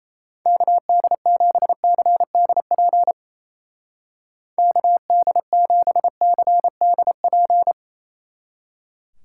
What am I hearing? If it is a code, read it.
Morse code, heard as a keyed tone, about 30 wpm, 700 Hz. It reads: KD7CDP KD7CDP
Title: Speed Racing - USA 2x3 Call Signs 28wpm